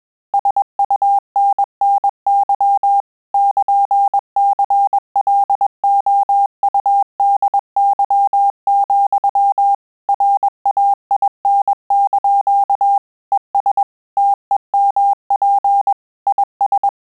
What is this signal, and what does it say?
Morse code, heard as a keyed tone, about 21 wpm, 800 Hz. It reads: SUNNY <KN>CLOUDY, RAIN) ES TEMP IS
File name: QSO CW 100 CPM